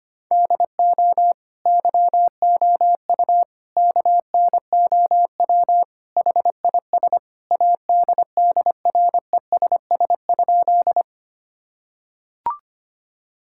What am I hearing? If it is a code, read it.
Morse code, heard as a keyed tone, about 25 wpm, 700 Hz. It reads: DO YOU KNOW HIS ADDRESS?
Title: Sentences from Top 1000 Words - Review Part 6 - 25wpm